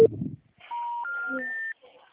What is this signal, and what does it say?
Special information tone, with speech.
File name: out- -2038- - - .17320